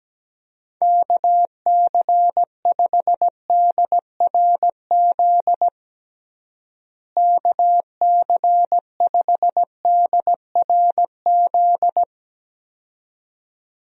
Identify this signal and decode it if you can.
Morse code, heard as a keyed tone, about 17 wpm, 700 Hz. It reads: KC5DRZ KC5DRZ
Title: USA Call Signs - 6 Characters - Repeated 2x - 17wpm